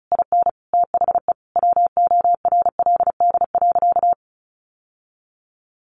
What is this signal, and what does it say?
Morse code, heard as a keyed tone, about 34 wpm, 700 Hz. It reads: IN THE WORLD.
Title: Sentences from Top 1100 Words 35wpm